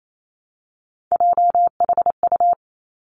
Morse code, heard as a keyed tone, about 28 wpm, 700 Hz. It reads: JHU
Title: Three Challenging Letter-Number Combinations 28wpm